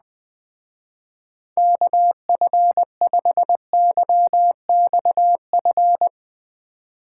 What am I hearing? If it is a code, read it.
Morse code, heard as a keyed tone, about 19 wpm, 700 Hz. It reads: KF5YXF
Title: Speed Racing - USA 2x3 Call Signs 17wpm